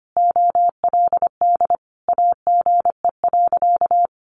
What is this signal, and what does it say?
Morse code, heard as a keyed tone, about 25 wpm, 700 Hz. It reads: OLD AGE.
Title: Sentences from Top 1300 Words 25wpm